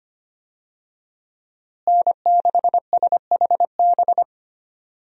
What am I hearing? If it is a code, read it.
Morse code, heard as a keyed tone, about 25 wpm, 700 Hz. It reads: N6SHB
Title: USA Call Signs - 5 Characters - Repeated 2x - 25wpm